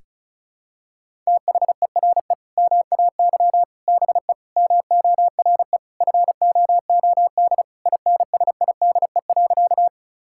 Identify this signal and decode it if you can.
Morse code, heard as a keyed tone, about 34 wpm, 700 Hz. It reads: THERE MAY BE MORE FOOD INSIDE.
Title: Sentences from Top 1000 Words - Review Part 6 - 35wpm